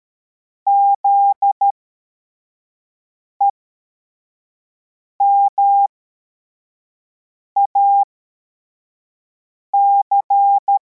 Morse code, heard as a keyed tone, about 13 wpm, spaced as for 4 wpm, 800 Hz. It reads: ZEMAC
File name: formation / cw / Text43